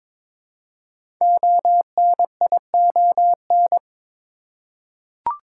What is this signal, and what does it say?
Morse code, heard as a keyed tone, about 22 wpm, 700 Hz. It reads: ONION
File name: 5 Letter Words - First 3 Letters Given - 22wpm